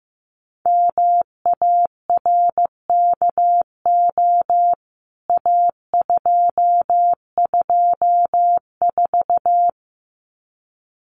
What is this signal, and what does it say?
Morse code, heard as a keyed tone, about 15 wpm, 700 Hz. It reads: MARKO A224
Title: CWT Exchange 15wpm